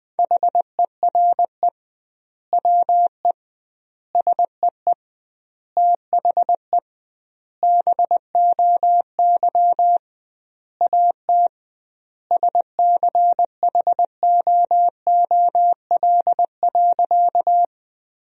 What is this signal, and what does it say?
Morse code, heard as a keyed tone, about 20 wpm, 700 Hz. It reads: HERE WE SEE THE BOY AT SCHOOL.
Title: 2x Word-Spacing - Sentences from Top 300 Words - 20wpm